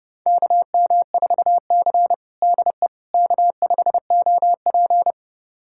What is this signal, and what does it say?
Morse code, heard as a keyed tone, about 32 wpm, 700 Hz. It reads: KM4C DE K5OP
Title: USA Callsign DE Callsign - 4 Characters - 30wpm